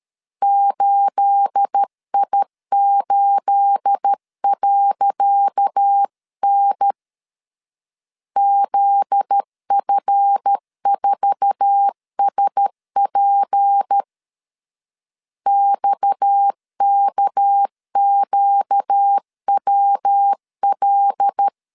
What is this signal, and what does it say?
Morse code, heard as a keyed tone, about 13 wpm, 800 Hz. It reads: 8I8.N ZF4SP XKQWL
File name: LEZIONI / 10 SIGNS / 10C DOT 12wpm 2451